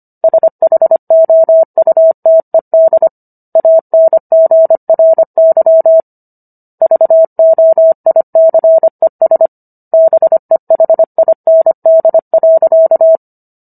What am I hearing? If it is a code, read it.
Morse code, heard as a keyed tone, about 25 wpm, 650 Hz. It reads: SHOUTED ANGRY VOICES BEHIND.